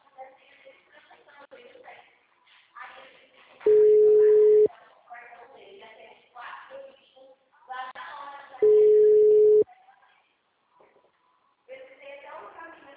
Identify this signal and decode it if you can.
Ringback tone (Europe, 425 Hz), with speech.